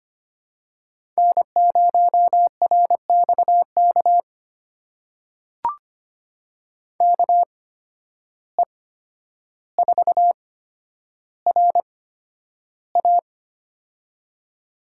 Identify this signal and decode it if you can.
Morse code, heard as a keyed tone, about 25 wpm, spaced as for 11 wpm, 700 Hz. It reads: N0RXK KE4RA
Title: USA Call Signs - 5 Characters - 8x Character Spacing - 25wpm